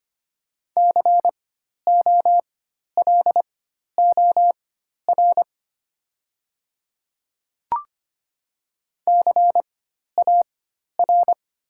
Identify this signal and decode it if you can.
Morse code, heard as a keyed tone, about 25 wpm, spaced as for 12 wpm, 690 Hz. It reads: COLOR CAR
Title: 4x Character Spacing - Top 500 Words - 25wpm